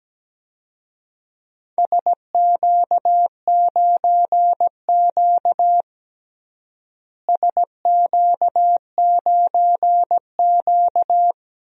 Morse code, heard as a keyed tone, about 17 wpm, 700 Hz. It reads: SQ9Q SQ9Q
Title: DXCC-100 Call Signs - 4 Characters - Repeated 2x - 17wpm